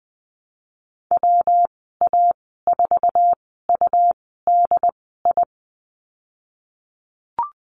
Morse code, heard as a keyed tone, about 20 wpm, spaced as for 15 wpm, 700 Hz. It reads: WA4UDI
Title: 193 Lesson 31g - Letter v - Call Signs 20wpm